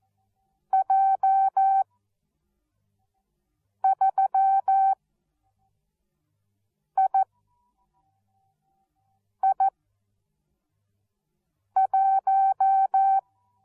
Morse code, heard as a keyed tone, about 14 wpm, spaced as for 4 wpm, 770 Hz. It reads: J3II1